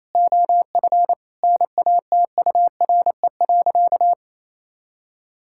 Morse code, heard as a keyed tone, about 28 wpm, 700 Hz. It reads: OF NATURE.
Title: Sentences from Top 1000 Words - Review Part 6 - 28wpm